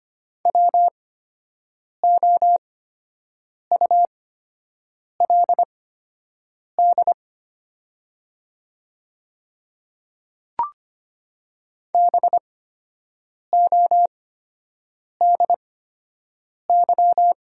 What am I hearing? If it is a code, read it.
Morse code, heard as a keyed tone, about 25 wpm, spaced as for 7 wpm, 700 Hz. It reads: WOULD BODY
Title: 8x Character Spacing - Top 500 Words - 25wpm